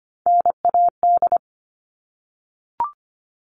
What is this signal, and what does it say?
Morse code, heard as a keyed tone, about 25 wpm, 700 Hz. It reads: NAD
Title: All Cut Numbers - 3-Digit 25wpm